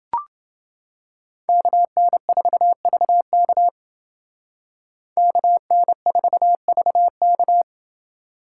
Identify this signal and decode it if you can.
Morse code, heard as a keyed tone, about 28 wpm, 700 Hz. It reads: KN4VK KN4VK